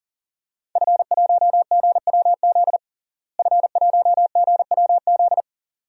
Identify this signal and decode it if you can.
Morse code, heard as a keyed tone, about 40 wpm, 700 Hz. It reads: F1GWZ F1GWZ
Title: DXCC-100 Call Signs - 5 Characters - Repeated 2x - 40wpm